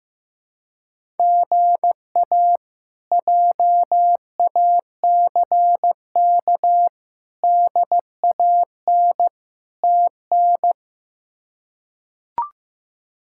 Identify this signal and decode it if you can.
Morse code, heard as a keyed tone, about 15 wpm, 700 Hz. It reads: GA JACK DAN TN
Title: K1USN SST QSO Practice - 15wpm